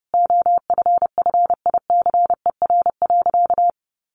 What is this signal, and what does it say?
Morse code, heard as a keyed tone, about 30 wpm, 700 Hz. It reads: OFFICER.
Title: Sentences from Top 1200 Words 30wpm